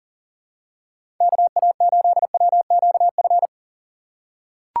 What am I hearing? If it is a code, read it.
Morse code, heard as a keyed tone, about 40 wpm, 700 Hz. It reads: KA8WQF
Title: USA Call Signs - 6 Characters - 3x Character Spacing - 40wpm